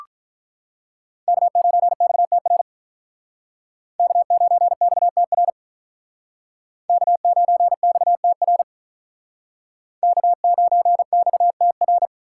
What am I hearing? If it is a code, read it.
Morse code, heard as a keyed tone, about 45 wpm, 700 Hz. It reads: K9XTR K9XTR K9XTR K9XTR